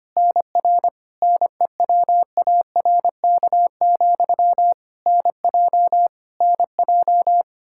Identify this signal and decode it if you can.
Morse code, heard as a keyed tone, about 25 wpm, 700 Hz. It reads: NR NEWARK, NJ NJ
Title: K1USN SST QSO Practice 25wpm